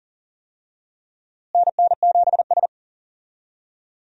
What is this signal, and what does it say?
Morse code, heard as a keyed tone, about 40 wpm, 700 Hz. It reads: NN7S
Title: USA Call Signs - 4 Characters - Repeated 2x - 40wpm